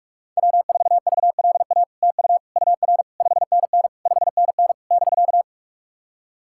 Morse code, heard as a keyed tone, about 47 wpm, 700 Hz. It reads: W4VLA TU UR 5NN 5NN <BK>
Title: SOTA QSO Practice 45wpm